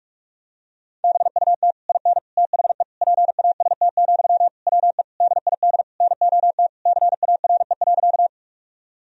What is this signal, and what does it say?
Morse code, heard as a keyed tone, about 46 wpm, 700 Hz. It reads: BUT IN THE PAST, WE DID NOT CARE.
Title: Sentences from Top 1000 Words - Review Part 3 - 45wpm